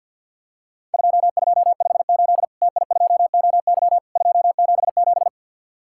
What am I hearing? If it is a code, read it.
Morse code, heard as a keyed tone, about 50 wpm, 700 Hz. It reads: 2258 TI2OY 277
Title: Worked All Europe DX Contest - QTC Exchange - 50wpm